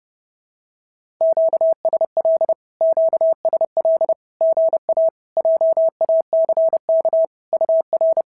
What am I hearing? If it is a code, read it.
Morse code, heard as a keyed tone, about 30 wpm, 650 Hz. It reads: QSL QSL GA JACK UR
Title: SOTA QSO Practice 30wpm